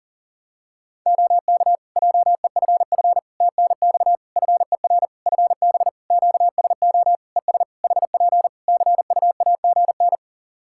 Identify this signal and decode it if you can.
Morse code, heard as a keyed tone, about 40 wpm, 700 Hz. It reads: OK JEFF TNX FER FB QSO ES HP CUAGN
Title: Practice QSO - Exchange 4 - Ending - 40wpm